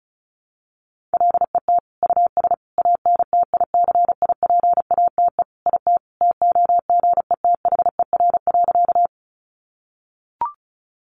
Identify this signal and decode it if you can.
Morse code, heard as a keyed tone, about 34 wpm, 700 Hz. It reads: LET US ANTICIPATE IT TOGETHER.
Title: Sentences from Top 1500 Words 35wpm